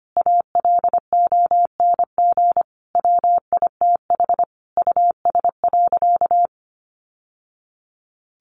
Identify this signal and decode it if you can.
Morse code, heard as a keyed tone, about 25 wpm, 700 Hz. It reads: ALONG WITH US.
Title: Sentences from Top 1200 Words 25wpm